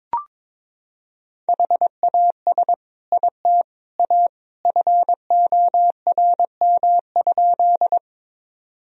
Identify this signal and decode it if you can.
Morse code, heard as a keyed tone, about 22 wpm, 700 Hz. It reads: HAS IT A FORM?
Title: Sentences from Top 1000 Words - Review Part 1 - 22wpm